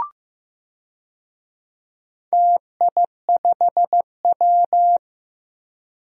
Morse code, heard as a keyed tone, about 15 wpm, 700 Hz. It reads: TI5W
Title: DXCC-100 Call Signs - 4 Characters - Repeated 2x - 15wpm